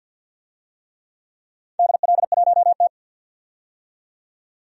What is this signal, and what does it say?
Morse code, heard as a keyed tone, about 50 wpm, 700 Hz. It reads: DL1T
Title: DXCC-100 Call Signs - 4 Characters - 50wpm